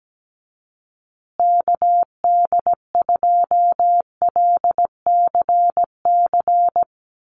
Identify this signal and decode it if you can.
Morse code, heard as a keyed tone, about 17 wpm, 700 Hz. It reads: KD2LCC
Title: USA Call Signs Repeated 2x 17wpm